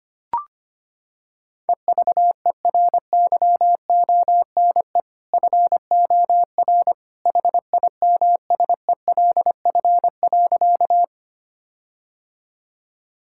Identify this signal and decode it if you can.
Morse code, heard as a keyed tone, about 25 wpm, 700 Hz. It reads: EVERYONE FOR HIMSELF.
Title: Sentences from Top 1600 Words 25wpm